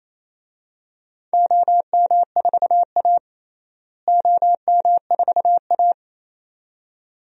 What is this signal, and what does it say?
Morse code, heard as a keyed tone, about 27 wpm, 700 Hz. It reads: OM4A OM4A
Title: DXCC-100 Call Signs - 4 Characters - Repeated 2x - 28wpm